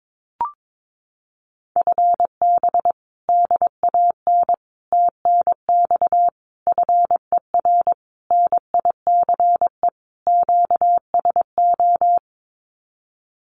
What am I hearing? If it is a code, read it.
Morse code, heard as a keyed tone, about 22 wpm, 700 Hz. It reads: FB DAN TNX FER NICE QSO
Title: Common QSO Phrases Encore 22wpm